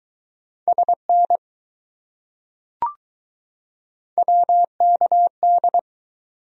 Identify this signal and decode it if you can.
Morse code, heard as a keyed tone, about 20 wpm, 700 Hz. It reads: SN WKD